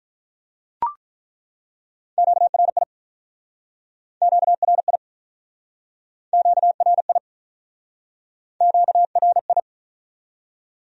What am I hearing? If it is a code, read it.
Morse code, heard as a keyed tone, about 49 wpm, 700 Hz. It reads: QRI QRI QRI QRI